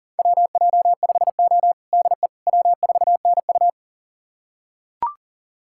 Morse code, heard as a keyed tone, about 40 wpm, 700 Hz. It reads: WJ5O DE W4NU
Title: USA Callsign DE Callsign - 4 Characters - 40wpm